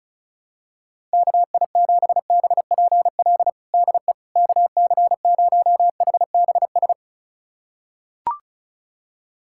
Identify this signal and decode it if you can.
Morse code, heard as a keyed tone, about 36 wpm, 700 Hz. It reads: KI7BPL DE KC0HBS
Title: USA Callsign DE Callsign - 6 Characters - 35wpm